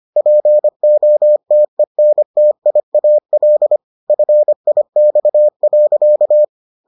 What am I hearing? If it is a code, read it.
Morse code, heard as a keyed tone, about 25 wpm, 600 Hz. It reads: POTENTIAL FIX.